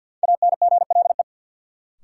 Morse code, heard as a keyed tone, about 51 wpm, 700 Hz. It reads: ANGLE
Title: Sentences from Top 1500 Words - New Word Drill - 50wpm